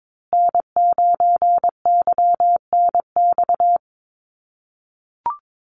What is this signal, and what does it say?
Morse code, heard as a keyed tone, about 22 wpm, 700 Hz. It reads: N9YNX
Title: USA Call Signs Repeated 2x 22wpm